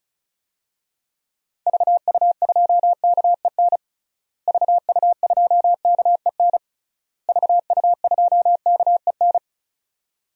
Morse code, heard as a keyed tone, about 34 wpm, 700 Hz. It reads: VU2KEN VU2KEN VU2KEN
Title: DXCC-100 Call Signs - 6 Characters - Repeated 3x - 35wpm